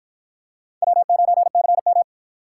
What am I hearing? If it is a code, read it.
Morse code, heard as a keyed tone, about 53 wpm, 700 Hz. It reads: W9CM